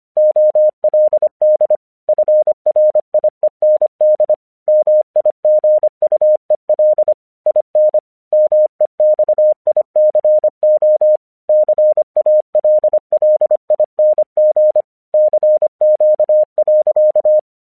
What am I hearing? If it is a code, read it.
Morse code, heard as a keyed tone, about 25 wpm, 610 Hz. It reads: OLD FRIEND MIGUEL IN MEXICO CALLING CQ.
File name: 1 English 25wpm / AQC30